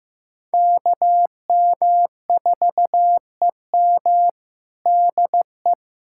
Morse code, heard as a keyed tone, about 15 wpm, 700 Hz. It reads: KM4EM DE
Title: Practice QSO - Exchange 4 - Ending - 15wpm